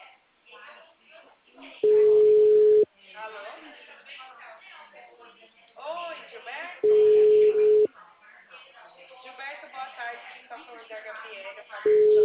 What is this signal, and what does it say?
Ringback tone (Europe, 425 Hz), with speech.
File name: out- -2004- - - .71952